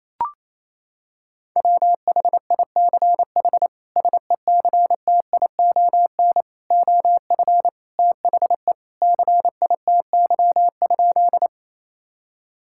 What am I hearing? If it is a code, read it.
Morse code, heard as a keyed tone, about 27 wpm, 700 Hz. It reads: WHICH SECTION OF THE CITY?
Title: Sentences from Top 1000 Words - Review Part 5 - 28wpm